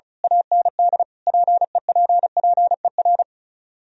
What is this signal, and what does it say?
Morse code, heard as a keyed tone, about 35 wpm, 700 Hz. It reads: AND PEPPER
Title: Speed-Racing - Sets of 5 Words 35wpm